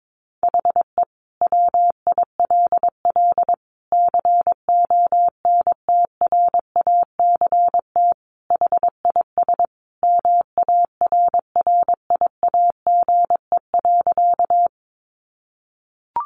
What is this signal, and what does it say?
Morse code, heard as a keyed tone, about 22 wpm, 700 Hz. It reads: HE WILL CONTRACT HIS MARRIAGE.
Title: Sentences from Top 1400 Words 22wpm